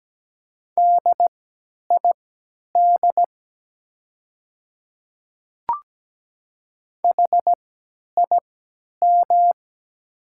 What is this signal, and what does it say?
Morse code, heard as a keyed tone, about 17 wpm, spaced as for 10 wpm, 700 Hz. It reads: DID HIM
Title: 3x Character Spacing - Top 100 Words - 17wpm